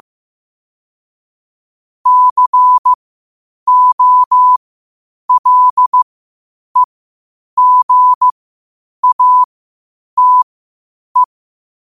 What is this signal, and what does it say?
Morse code, heard as a keyed tone, about 15 wpm, spaced as for 8 wpm, 1000 Hz. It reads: COLEGATE
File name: qotdfiles / 8wpm